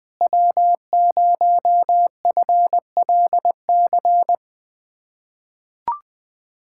Morse code, heard as a keyed tone, about 21 wpm, 700 Hz. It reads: W0FLC
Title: USA Callsign DE Callsign - 5 Characters - 20wpm